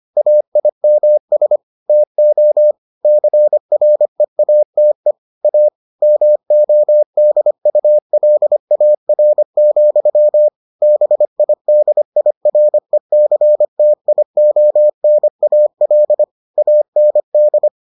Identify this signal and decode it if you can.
Morse code, heard as a keyed tone, about 25 wpm, 600 Hz. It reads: AIMS TO CREATE A MODULAR, BIDIRECTIONAL AND